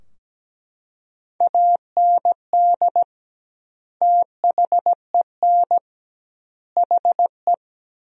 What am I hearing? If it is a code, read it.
Morse code, heard as a keyed tone, about 17 wpm, 700 Hz. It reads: AND THEN HE
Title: 2x Word-Spacing - Sentences from Top 400 Words - 17wpm